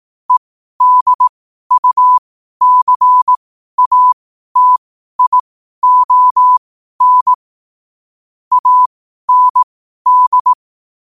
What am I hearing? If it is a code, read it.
Morse code, heard as a keyed tone, about 18 wpm, spaced as for 12 wpm, 1000 Hz. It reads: EDUCATION AND
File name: qotdfiles / 12wpm